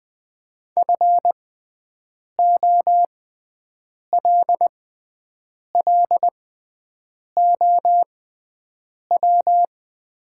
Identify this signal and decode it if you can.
Morse code, heard as a keyed tone, about 20 wpm, spaced as for 7 wpm, 700 Hz. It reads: FOLLOW
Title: 6x Character Spacing - Top 500 Words - 20wpm